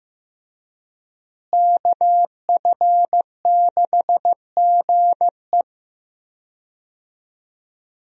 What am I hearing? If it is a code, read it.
Morse code, heard as a keyed tone, about 15 wpm, 700 Hz. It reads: KF6GE
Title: USA Call Signs - 5 Characters - Repeated 2x - 15wpm